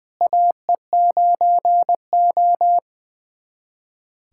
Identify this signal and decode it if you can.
Morse code, heard as a keyed tone, about 20 wpm, 700 Hz. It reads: AE9O
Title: USA Callsign DE Callsign - 4 Characters - 20wpm